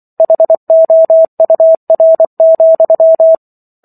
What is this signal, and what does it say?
Morse code, heard as a keyed tone, about 24 wpm, 650 Hz. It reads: HOUR,